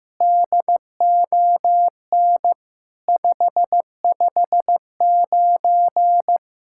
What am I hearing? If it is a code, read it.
Morse code, heard as a keyed tone, about 15 wpm, 700 Hz. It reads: DON 559
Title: SOTA QSO Practice 15wpm